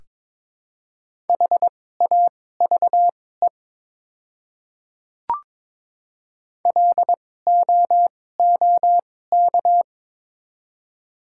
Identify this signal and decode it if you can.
Morse code, heard as a keyed tone, about 22 wpm, spaced as for 16 wpm, 700 Hz. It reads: HAVE LOOK
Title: 2x Character Spacing - Top 100 Words - 22wpm